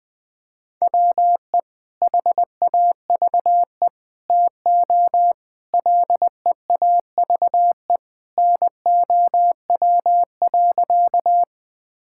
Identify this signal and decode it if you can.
Morse code, heard as a keyed tone, about 20 wpm, 700 Hz. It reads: WE HAVE TO LEAVE NOW.
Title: Sentences with 3-Word Phrases - 20wpm